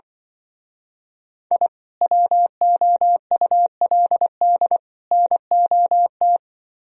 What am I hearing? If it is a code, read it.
Morse code, heard as a keyed tone, about 24 wpm, 700 Hz. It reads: I WOULD NOT
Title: Speed-Racing - Sets of 3 Words 20wpm